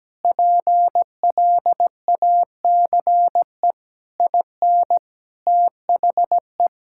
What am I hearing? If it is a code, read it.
Morse code, heard as a keyed tone, about 17 wpm, 700 Hz. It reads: PLACE IN THE
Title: Sentences from Top 1600 Words 17wpm